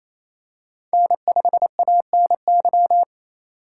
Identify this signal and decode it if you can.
Morse code, heard as a keyed tone, about 28 wpm, 700 Hz. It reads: N5ANY
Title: USA Call Signs - 5 Characters - 2x Character Spacing - 28wpm